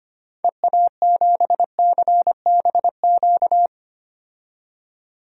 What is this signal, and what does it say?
Morse code, heard as a keyed tone, about 25 wpm, 700 Hz. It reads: EA7CBQ
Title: DXCC-100 Call Signs - 6 Characters - 25wpm